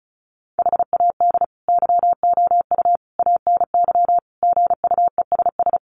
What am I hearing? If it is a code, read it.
Morse code, heard as a keyed tone, about 34 wpm, 700 Hz. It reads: HAD YOU ANY GUESS
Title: Sentences from Top 1500 Words 35wpm